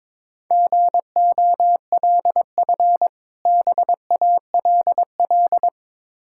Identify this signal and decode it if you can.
Morse code, heard as a keyed tone, about 23 wpm, 690 Hz. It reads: GOLF BALL
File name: Difficult Words with Hints - 22wpm